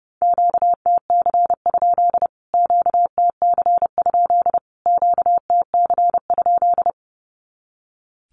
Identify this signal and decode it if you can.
Morse code, heard as a keyed tone, about 30 wpm, 700 Hz. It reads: QTC? QTC? QTC?
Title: Q-Codes Repeated 3x 30wpm